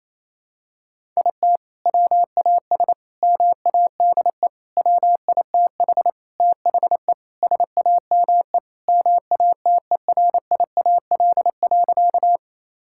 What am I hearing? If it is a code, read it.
Morse code, heard as a keyed tone, about 28 wpm, 700 Hz. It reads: IT WAS MADE WITH THE SAME MATERIAL.
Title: Sentences from Top 1000 Words - Review Part 5 - 28wpm